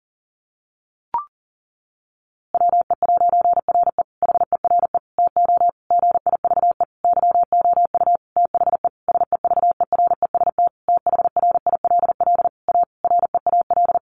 Morse code, heard as a keyed tone, about 40 wpm, 700 Hz. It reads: WE'RE HERE TO GIVE YOU THE SEVEREST THRILL A REAL
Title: A Sound Of Thunder 40wpm